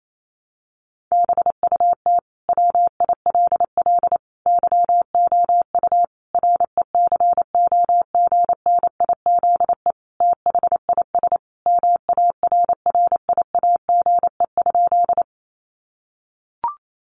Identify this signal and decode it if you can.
Morse code, heard as a keyed tone, about 28 wpm, 700 Hz. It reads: BUT WILL YOU RECOGNIZE THIS MARRIAGE?
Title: Sentences from Top 1400 Words 28wpm